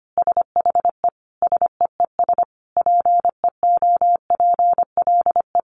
Morse code, heard as a keyed tone, about 25 wpm, 700 Hz. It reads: SHE SEES PEOPLE
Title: Easy Phrases 25wpm